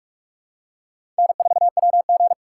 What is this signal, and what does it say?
Morse code, heard as a keyed tone, about 44 wpm, 700 Hz. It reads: N4WG
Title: USA Call Signs - 2x Character Spacing - 45wpm